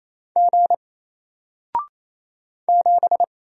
Morse code, heard as a keyed tone, about 27 wpm, 700 Hz. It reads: G 7
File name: Single-Letter-Number-28wpm